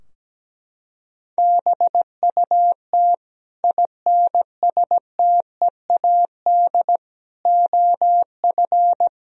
Morse code, heard as a keyed tone, about 17 wpm, 700 Hz. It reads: BUT INSTEAD OF
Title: Sentences from Top 1000 Words - Review Part 4 - 17wpm